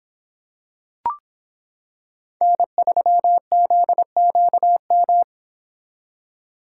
Morse code, heard as a keyed tone, about 26 wpm, 700 Hz. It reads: N3ZQM